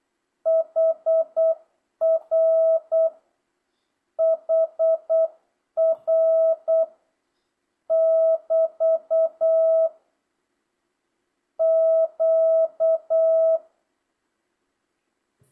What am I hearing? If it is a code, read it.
Morse code, heard as a keyed tone, about 8 wpm, 650 Hz. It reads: HR HR <BT> Q
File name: jh8cbh / 2025cw